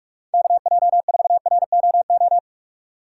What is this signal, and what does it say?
Morse code, heard as a keyed tone, about 45 wpm, 700 Hz. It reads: KJ4ROO
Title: USA Call Signs - 6 Characters - 4x Character Spacing - 45wpm